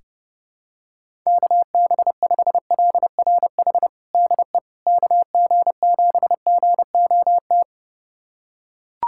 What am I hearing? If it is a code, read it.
Morse code, heard as a keyed tone, about 30 wpm, 700 Hz. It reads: KB5LRH DE KG7GOT
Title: USA Callsign DE Callsign 30wpm